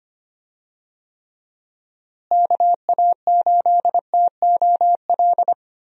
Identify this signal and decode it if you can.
Morse code, heard as a keyed tone, about 25 wpm, 700 Hz. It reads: KA8TOL
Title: USA Call Signs - 6 Characters - Repeated 3x - 25wpm